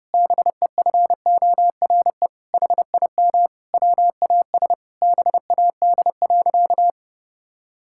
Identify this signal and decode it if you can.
Morse code, heard as a keyed tone, about 30 wpm, 700 Hz. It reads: BEFORE HIM WAS BAD.
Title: Sentences from Top 1600 Words 30wpm